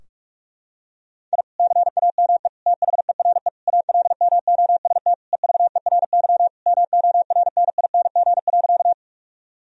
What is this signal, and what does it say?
Morse code, heard as a keyed tone, about 41 wpm, 700 Hz. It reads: I CAME THERE ALMOST EVERY MORNING.
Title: Sentences from Top 1000 Words - Review Part 4 - 45wpm